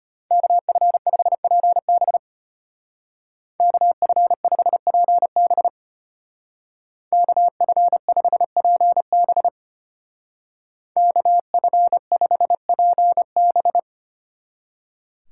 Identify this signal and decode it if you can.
Morse code, heard as a keyed tone, about 32 wpm, 700 Hz. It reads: KF5PB KF5PB KF5PB KF5PB